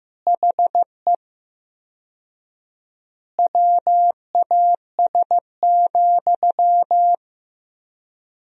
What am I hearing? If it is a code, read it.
Morse code, heard as a keyed tone, about 15 wpm, 700 Hz. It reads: HE WAS,
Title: 4x Word-Spacing - Sentences from Top 400 Words - 15wpm